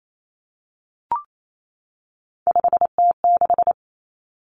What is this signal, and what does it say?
Morse code, heard as a keyed tone, about 28 wpm, 700 Hz. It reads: 5T6
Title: Common Cut Numbers - 3-Digit 28wpm